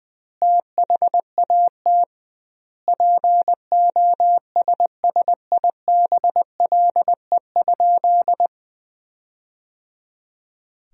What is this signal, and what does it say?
Morse code, heard as a keyed tone, about 20 wpm, 700 Hz. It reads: THAT POSSIBLE?
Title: 2x Word-Spacing - Sentences from Top 500 Words - 20wpm